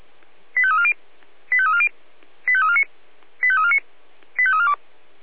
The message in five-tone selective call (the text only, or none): 85340 85340 85349 85349 85342